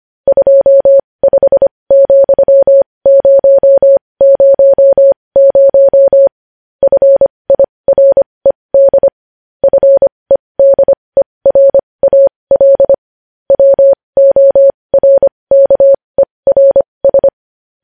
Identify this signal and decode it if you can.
Morse code, heard as a keyed tone, about 25 wpm, spaced as for 20 wpm, 550 Hz. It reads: 25,000 FIRED FEDERAL WORKERS
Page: news-headlines-in-morse-code-at-20-wpm